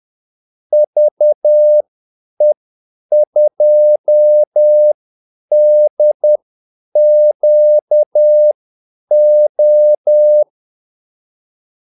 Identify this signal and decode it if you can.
Morse code, heard as a keyed tone, about 10 wpm, spaced as for 8 wpm, 600 Hz. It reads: VE2DQO
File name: Callsigns-10-8wpm